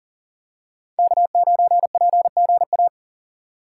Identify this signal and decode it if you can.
Morse code, heard as a keyed tone, about 40 wpm, 700 Hz. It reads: K9PGA
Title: USA Call Signs - 5 Characters - 2x Character Spacing - 40wpm